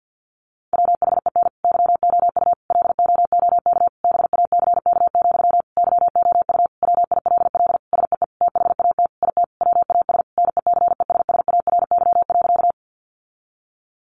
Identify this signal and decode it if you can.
Morse code, heard as a keyed tone, about 50 wpm, 700 Hz. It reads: WHEN YOU LOOK BACK, YOU WILL SEE THAT IT WAS NECESSARY.
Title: Sentences from Top 800 Words 50wpm